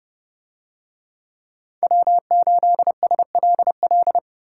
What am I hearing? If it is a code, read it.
Morse code, heard as a keyed tone, about 29 wpm, 700 Hz. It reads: W8SLL